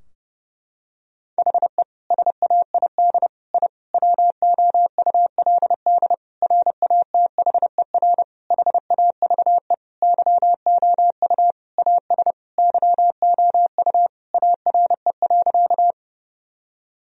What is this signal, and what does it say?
Morse code, heard as a keyed tone, about 30 wpm, 700 Hz. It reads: HE SAID I WOULD RATHER HAVE YOU AS YOU ARE.
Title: Sentences from Top 1000 Words - Review Part 4 - 30wpm